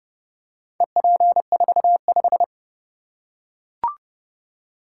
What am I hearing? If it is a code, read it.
Morse code, heard as a keyed tone, about 29 wpm, 700 Hz. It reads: EP45